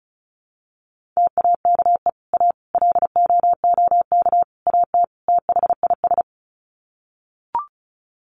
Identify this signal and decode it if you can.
Morse code, heard as a keyed tone, about 34 wpm, 700 Hz. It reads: TAKE A LOOK AT THIS
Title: Sets of 5 Words 35wpm